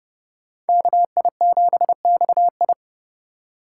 Morse code, heard as a keyed tone, about 30 wpm, 700 Hz. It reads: KI7XI